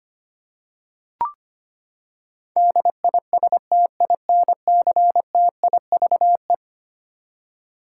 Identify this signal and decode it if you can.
Morse code, heard as a keyed tone, about 25 wpm, 700 Hz. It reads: DISTINCTIVE
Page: morse-code-ninja-practice-25wpm